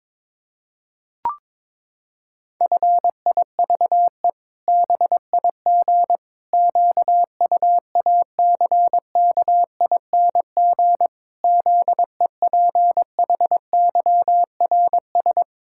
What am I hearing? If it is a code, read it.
Morse code, heard as a keyed tone, about 22 wpm, 700 Hz. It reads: FIVE BIG QUACKING ZEPHYRS
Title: Pangram Challenge v2 - 22wpm